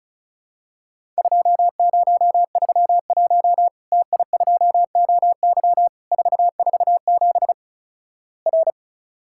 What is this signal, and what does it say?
Morse code, heard as a keyed tone, about 36 wpm, 700 Hz. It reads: 2031 TI2OY 447 R
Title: Worked All Europe DX Contest - QTC Exchange - 35wpm